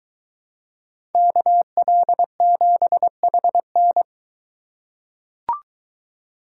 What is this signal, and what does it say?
Morse code, heard as a keyed tone, about 22 wpm, 700 Hz. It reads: KL7HN